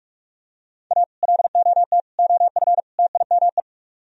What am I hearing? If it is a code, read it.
Morse code, heard as a keyed tone, about 46 wpm, 700 Hz. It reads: A LOT OF TIME